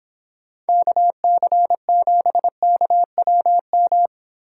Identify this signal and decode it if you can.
Morse code, heard as a keyed tone, about 26 wpm, 700 Hz. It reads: KC7KWM